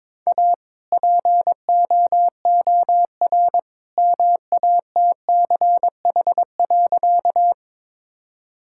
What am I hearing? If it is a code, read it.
Morse code, heard as a keyed tone, about 22 wpm, 700 Hz. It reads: A POOR MATCH.
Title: Sentences from Top 1600 Words 22wpm